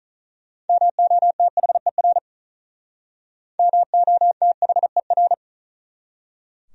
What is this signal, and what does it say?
Morse code, heard as a keyed tone, about 46 wpm, 700 Hz. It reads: MOTHER MOTHER